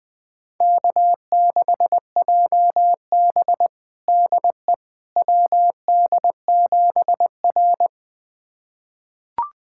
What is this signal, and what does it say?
Morse code, heard as a keyed tone, about 21 wpm, 700 Hz. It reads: K6JB DE WD7R
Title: USA Callsign DE Callsign - 4 Characters - 20wpm